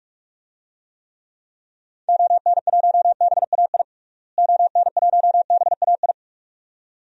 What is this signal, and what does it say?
Morse code, heard as a keyed tone, about 41 wpm, 700 Hz. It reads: ON1BAI ON1BAI
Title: DXCC-100 Call Signs - 6 Characters - Repeated 2x - 45wpm